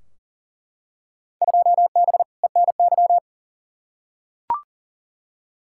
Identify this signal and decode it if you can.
Morse code, heard as a keyed tone, about 40 wpm, 700 Hz. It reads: 2B ENY
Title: Field Day Exchanges Encore - 40wpm